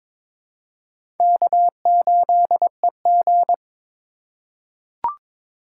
Morse code, heard as a keyed tone, about 21 wpm, 700 Hz. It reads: K8EG
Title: USA Call Signs - 4 Characters - 3x Character Spacing - 22wpm